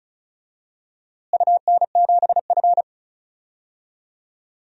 Morse code, heard as a keyed tone, about 36 wpm, 700 Hz. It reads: UN7F
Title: DXCC-100 Call Signs - 4 Characters - Repeated 2x - 35wpm